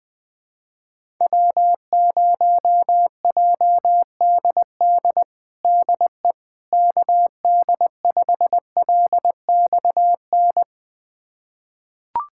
Morse code, heard as a keyed tone, about 20 wpm, 700 Hz. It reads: W0JDD DE KD5LXN
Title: USA Callsign DE Callsign 20wpm